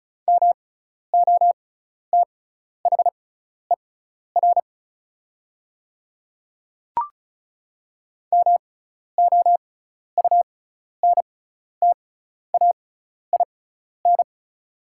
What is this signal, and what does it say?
Morse code, heard as a keyed tone, about 34 wpm, spaced as for 12 wpm, 700 Hz. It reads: MOTHER MOUNTAIN
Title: 6x Character Spacing - Top 300 Words - 35wpm